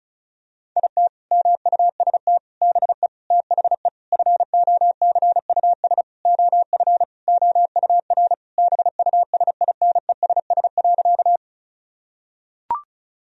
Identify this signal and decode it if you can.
Morse code, heard as a keyed tone, about 36 wpm, 700 Hz. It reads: IT MUST BE THE FOCUS OF OUR BUSINESS.
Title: Sentences from Top 1000 Words - Review Part 4 - 35wpm